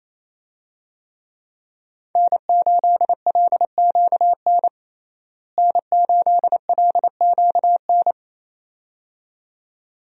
Morse code, heard as a keyed tone, about 28 wpm, 700 Hz. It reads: N8LQN N8LQN
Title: USA Call Signs - 5 Characters - Repeated 2x - 28wpm